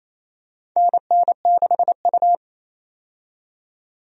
Morse code, heard as a keyed tone, about 28 wpm, 700 Hz. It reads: NN6U
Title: USA Call Signs - 4 Characters - 28wpm